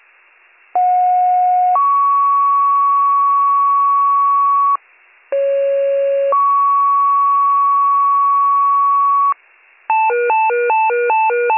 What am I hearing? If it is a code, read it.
Two-tone page, heard as a one-second tone, then a three-second tone, about 730 then 1120 Hz, about 550 then 1120 Hz. It is heard from 0.7 to 9.3 s.